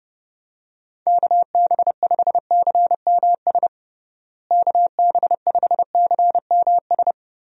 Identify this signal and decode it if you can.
Morse code, heard as a keyed tone, about 30 wpm, 700 Hz. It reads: KB5CMS KB5CMS
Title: USA Call Signs - 6 Characters - Repeated 3x - 30wpm